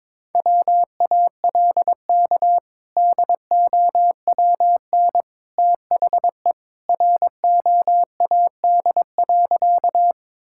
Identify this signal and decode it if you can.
Morse code, heard as a keyed tone, about 21 wpm, 700 Hz. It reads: WALK DOWN THE ROAD.
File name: practice / 22wpm / Sentences with 2-Word Phrases - 22wpm